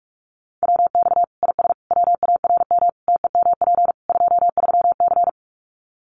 Morse code, heard as a keyed tone, about 44 wpm, 700 Hz. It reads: WX IS WARM TEMP 23C
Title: Common QSO Phrases 45wpm